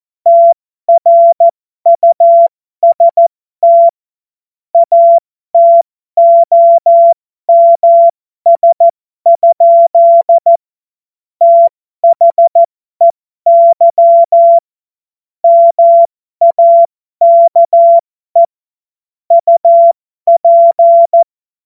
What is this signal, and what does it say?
Morse code, heard as a keyed tone, about 14 wpm, spaced as for 12 wpm, 680 Hz. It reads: TRUST ATOMS? THEY MAKE UP
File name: Morsekurs / Witze